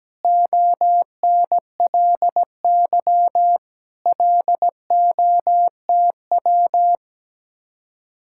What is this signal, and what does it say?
Morse code, heard as a keyed tone, about 17 wpm, 700 Hz. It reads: ONLY LOTW
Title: Speed-Racing Common QSO Phrases - 17wpm